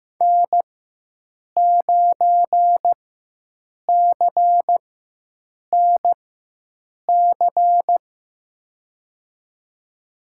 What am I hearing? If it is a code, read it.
Morse code, heard as a keyed tone, about 15 wpm, spaced as for 7 wpm, 700 Hz. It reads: N9CNC
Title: USA Call Signs - 5 Characters - 4x Character Spacing - 15wpm